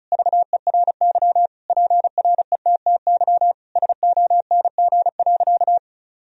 Morse code, heard as a keyed tone, about 36 wpm, 700 Hz. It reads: VERY PRETTY SONG.
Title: Sentences from Top 1000 Words - Review Part 2 - 35wpm